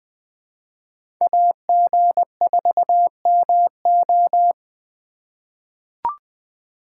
Morse code, heard as a keyed tone, about 20 wpm, 700 Hz. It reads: AG4MO
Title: USA Call Signs - 5 Characters - 4x Character Spacing - 20wpm